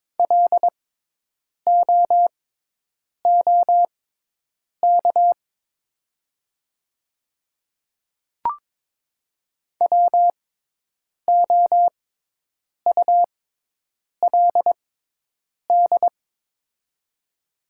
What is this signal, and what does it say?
Morse code, heard as a keyed tone, about 22 wpm, spaced as for 8 wpm, 700 Hz. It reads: LOOK WOULD
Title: 6x Character Spacing - Top 300 Words - 22wpm